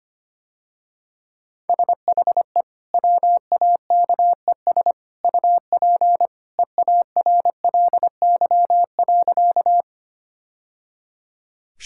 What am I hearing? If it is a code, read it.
Morse code, heard as a keyed tone, about 25 wpm, 700 Hz. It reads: SHE WAKES UP EARLY.
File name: Easy Sentences with Hints - 25wpm